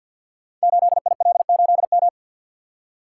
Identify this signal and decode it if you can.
Morse code, heard as a keyed tone, about 48 wpm, 700 Hz. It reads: 8IL8M
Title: Code Groups - 5 Letter:Numbers - 50wpm